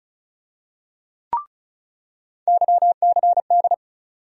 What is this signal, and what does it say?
Morse code, heard as a keyed tone, about 36 wpm, 700 Hz. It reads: YCD
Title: USA Call Sign Postfixes - 3 Letters 35wpm